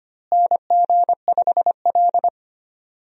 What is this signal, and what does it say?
Morse code, heard as a keyed tone, about 25 wpm, 700 Hz. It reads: NG5L
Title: USA Call Signs - 4 Characters - Repeated 2x - 25wpm